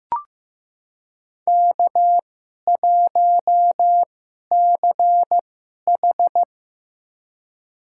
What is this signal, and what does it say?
Morse code, heard as a keyed tone, about 15 wpm, spaced as for 11 wpm, 690 Hz. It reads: K1CH
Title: USA Call Signs - 4 Characters - 2x Character Spacing - 15wpm